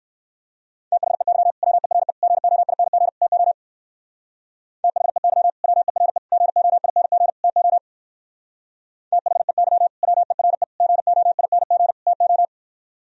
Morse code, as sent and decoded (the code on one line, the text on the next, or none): - .... -.-- / .-- . .-. / --. --- .. -. --. / - --- / - .... -.-- / .-- . .-. / --. --- .. -. --. / - --- / - .... -.-- / .-- . .-. / --. --- .. -. --. / - ---
THEY WERE GOING TO THEY WERE GOING TO THEY WERE GOING TO